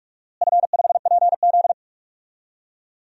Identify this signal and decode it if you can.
Morse code, heard as a keyed tone, about 45 wpm, 700 Hz. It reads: F5PZ
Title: DXCC-100 Call Signs - 4 Characters - Repeated 3x - 45wpm